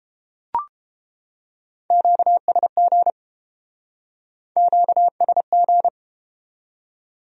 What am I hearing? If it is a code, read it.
Morse code, heard as a keyed tone, about 29 wpm, 700 Hz. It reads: QSG QSG